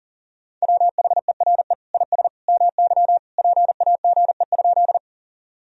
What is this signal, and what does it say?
Morse code, heard as a keyed tone, about 40 wpm, 700 Hz. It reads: WHERE IS MY PAGE?
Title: Sentences from Top 1000 Words - Review Part 1 - 40wpm